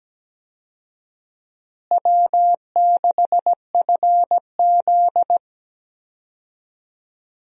Morse code, heard as a keyed tone, about 17 wpm, 700 Hz. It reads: W6FZ
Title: USA Call Signs - 4 Characters - Repeated 2x - 17wpm